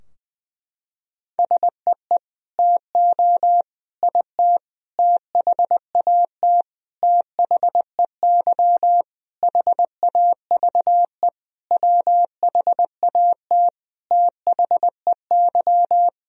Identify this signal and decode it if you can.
Morse code, heard as a keyed tone, about 20 wpm, 700 Hz. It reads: SEE TO IT THAT THEY HAVE WHAT THEY
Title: Sentences from Top 1000 Words - Review Part 2 - 20wpm